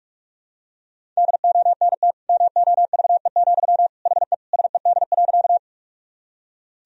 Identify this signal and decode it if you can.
Morse code, heard as a keyed tone, about 46 wpm, 700 Hz. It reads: DONT MOVE, HE SED.
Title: Sentences from Top 300 Words and Abbreviations 45wpm